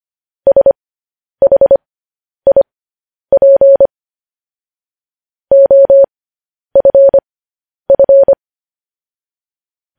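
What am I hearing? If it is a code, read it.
Morse code, heard as a keyed tone, about 25 wpm, spaced as for 10 wpm, 550 Hz. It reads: SHIP OFF
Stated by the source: Morse code news headlines practice transmission.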